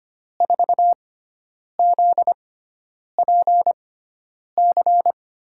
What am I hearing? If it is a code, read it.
Morse code, heard as a keyed tone, about 25 wpm, spaced as for 9 wpm, 700 Hz. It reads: 4ZPC
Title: USA Call Signs - 6 Characters - 6x Character Spacing - 25wpm